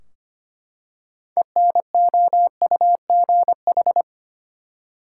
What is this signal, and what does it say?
Morse code, heard as a keyed tone, about 25 wpm, 700 Hz. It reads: ENOUGH
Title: Six Letter Words - 25wpm